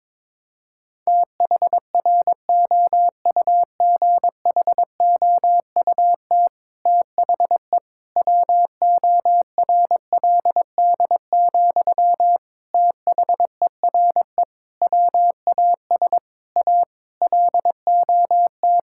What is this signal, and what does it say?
Morse code, heard as a keyed tone, about 22 wpm, 700 Hz. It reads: THROUGHOUT THE WORLD, THERE WAS A LOT
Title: Sentences from Top 1000 Words - Review Part 5 - 22wpm